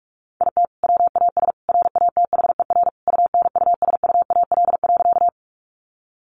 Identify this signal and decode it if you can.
Morse code, heard as a keyed tone, about 44 wpm, 700 Hz. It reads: IT WAS RATHER UNUSUAL.
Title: Sentences from Top 1400 Words 45wpm